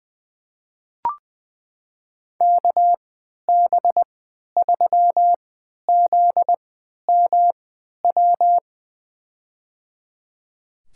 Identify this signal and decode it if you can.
Morse code, heard as a keyed tone, about 21 wpm, spaced as for 11 wpm, 700 Hz. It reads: KB3ZMW
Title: USA Call Signs - 6 Characters - 3x Character Spacing - 20wpm